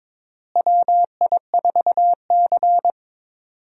Morse code, heard as a keyed tone, about 22 wpm, 700 Hz. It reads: WI4C
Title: USA Call Signs - 4 Characters - Repeated 3x - 22wpm